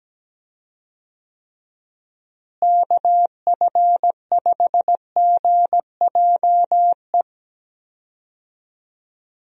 Morse code, heard as a keyed tone, about 17 wpm, 700 Hz. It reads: KF5GJE
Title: USA Call Signs - 6 Characters - Repeated 3x - 17wpm